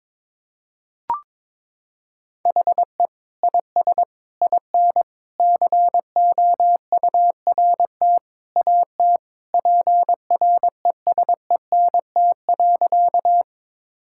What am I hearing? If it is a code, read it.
Morse code, heard as a keyed tone, about 22 wpm, 700 Hz. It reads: HE IS IN COURT AT PRESENT.
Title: Sentences from Top 1600 Words 22wpm